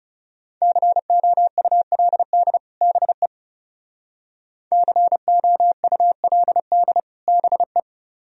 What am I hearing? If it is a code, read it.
Morse code, heard as a keyed tone, about 37 wpm, 700 Hz. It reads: COULD BE COULD BE
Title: Speed-Racing - Sets of 2 Words 30wpm